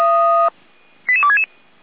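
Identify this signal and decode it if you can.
Five-tone selective call (ZVEI), heard as quick runs of five tones, tones of about 70 ms. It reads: 88266
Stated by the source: ZVEI five-tone selective call received with an SDR.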